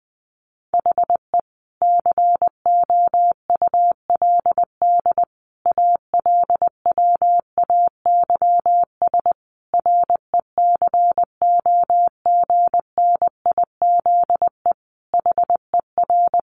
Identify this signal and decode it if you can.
Morse code, heard as a keyed tone, about 20 wpm, 700 Hz. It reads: HE COULD ALWAYS RECOGNIZE HER
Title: Sentences from Top 1400 Words 20wpm